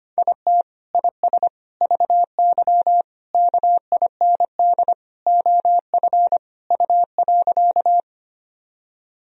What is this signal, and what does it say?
Morse code, heard as a keyed tone, about 25 wpm, 700 Hz. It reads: IT IS VY KIND OF U.
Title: Sentences from Top 200 Words and Abbreviations 25wpm